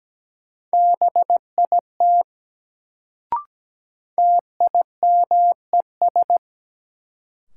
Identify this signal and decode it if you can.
Morse code, heard as a keyed tone, about 17 wpm, 700 Hz. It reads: BIT TIMES
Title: Sets of 3 Words - New Word Drill - 17wpm